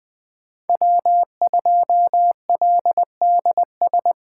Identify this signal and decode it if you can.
Morse code, heard as a keyed tone, about 20 wpm, 700 Hz. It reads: W2LDS